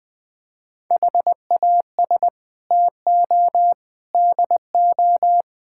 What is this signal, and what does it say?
Morse code, heard as a keyed tone, about 21 wpm, 700 Hz. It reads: HAS TO DO
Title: Build A Phrase - 5-Word Phrases - 20wpm